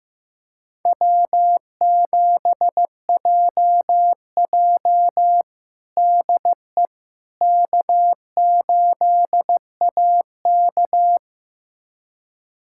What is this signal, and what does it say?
Morse code, heard as a keyed tone, about 16 wpm, 700 Hz. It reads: W7JJ DE K8AK
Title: USA Callsign DE Callsign - 4 Characters - 15wpm